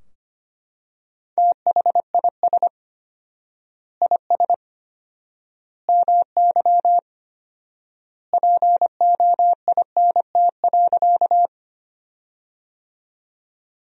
Morse code, heard as a keyed tone, about 25 wpm, 700 Hz. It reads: THIS IS MY POINT.
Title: 4x Word-Spacing - Sentences from Top 200 Words - 25wpm